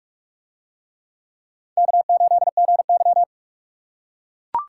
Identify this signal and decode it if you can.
Morse code, heard as a keyed tone, about 40 wpm, 700 Hz. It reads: K8GY